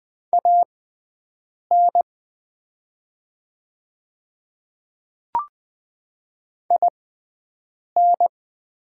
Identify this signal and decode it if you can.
Morse code, heard as a keyed tone, about 20 wpm, spaced as for 7 wpm, 700 Hz. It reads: AN IN
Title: 6x Character Spacing - Top 100 Words - 20wpm